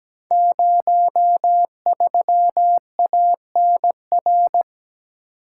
Morse code, heard as a keyed tone, about 17 wpm, 700 Hz. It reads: 03ANR
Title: Code Groups - 5 Letter:Numbers - 17wpm